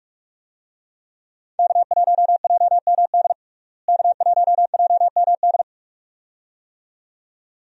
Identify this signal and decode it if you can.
Morse code, heard as a keyed tone, about 46 wpm, 700 Hz. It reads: K1JMD K1JMD
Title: USA Call Signs - 5 Characters - Repeated 2x - 45wpm